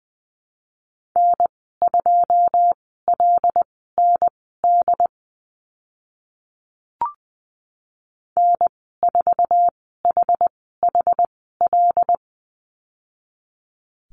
Morse code, heard as a keyed tone, about 20 wpm, spaced as for 14 wpm, 700 Hz. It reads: N2LND N4HHL
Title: 082 Lesson 14g - Number 2 - Call Signs 20wpm